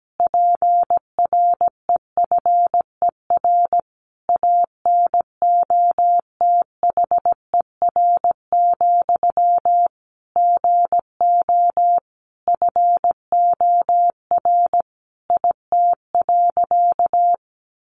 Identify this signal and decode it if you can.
Morse code, heard as a keyed tone, about 17 wpm, 700 Hz. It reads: PREFER ANOTHER, GO FOR IT.
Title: Sentences from Top 1100 Words 17wpm